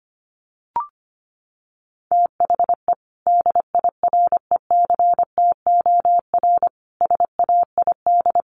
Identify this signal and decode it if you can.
Morse code, heard as a keyed tone, about 25 wpm, 700 Hz. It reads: THE DIRECTOR SAID
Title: Sentences from Top 1300 Words 25wpm